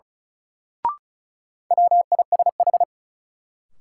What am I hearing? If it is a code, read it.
Morse code, heard as a keyed tone, about 36 wpm, 700 Hz. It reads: WISH
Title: Sentences from Top 1600 Words - New Word Drill - 35wpm